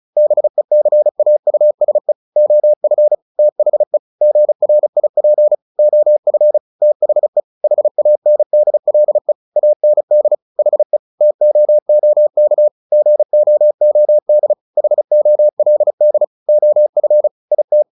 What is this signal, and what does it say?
Morse code, heard as a keyed tone, about 34 wpm, 600 Hz. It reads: BECAUSE OF THE GRIP OF THE HANDLE AND HE TOOK GOOD HOLD OF IT
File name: the old man and the sea 2 35wpm